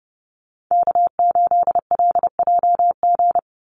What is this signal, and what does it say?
Morse code, heard as a keyed tone, about 30 wpm, 700 Hz. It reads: K8LJG
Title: NA SOTA Chaser Callsigns 30wpm